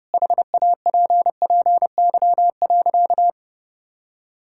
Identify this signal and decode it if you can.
Morse code, heard as a keyed tone, about 30 wpm, 700 Hz. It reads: HAPPY.
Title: Sentences from Top 1000 Words - Review Part 5 - 30wpm